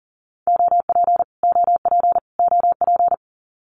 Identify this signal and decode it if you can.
Morse code, heard as a keyed tone, about 40 wpm, 700 Hz. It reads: OP OP OP
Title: Abbreviations Repeated 3x - 40wpm